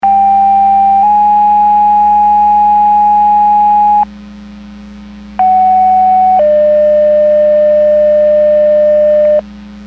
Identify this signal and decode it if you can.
Two-tone page, heard as a one-second tone, then a three-second tone, about 790 then 830 Hz, about 750 then 600 Hz. It is heard from 0.0 to 9.4 s.
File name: Portage 154.385 Paging